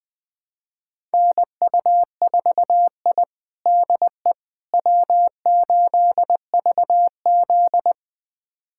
Morse code, heard as a keyed tone, about 21 wpm, 700 Hz. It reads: NU4I DE W8VZ
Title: USA Callsign DE Callsign - 4 Characters - 20wpm